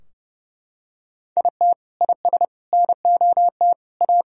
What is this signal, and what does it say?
Morse code, heard as a keyed tone, about 32 wpm, 700 Hz. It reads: IT IS NOT A
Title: Sentences from Top 1000 Words - Review Part 2 - 30wpm